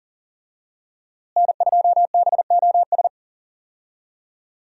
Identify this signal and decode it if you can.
Morse code, heard as a keyed tone, about 40 wpm, 700 Hz. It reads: N2BOS
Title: USA Call Signs - 5 Characters - Repeated 3x - 40wpm